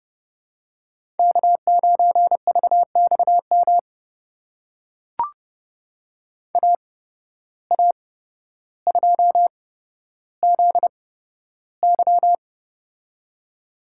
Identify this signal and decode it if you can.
Morse code, heard as a keyed tone, about 29 wpm, spaced as for 13 wpm, 700 Hz. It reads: K9VXM AA2ZY
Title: USA Call Signs - 5 Characters - 8x Character Spacing - 30wpm